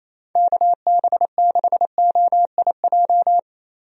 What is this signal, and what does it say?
Morse code, heard as a keyed tone, about 28 wpm, 700 Hz. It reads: KB6OIJ
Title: USA Call Signs - 6 Characters - 28wpm